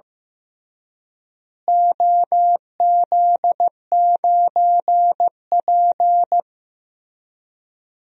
Morse code, heard as a keyed tone, about 15 wpm, 700 Hz. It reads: OZ9P
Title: DXCC-100 Call Signs - 4 Characters - Repeated 3x - 15wpm